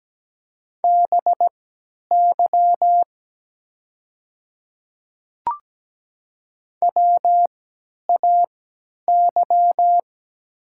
Morse code, heard as a keyed tone, about 17 wpm, spaced as for 10 wpm, 700 Hz. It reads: BY WAY
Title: 3x Character Spacing - Top 200 Words - 17wpm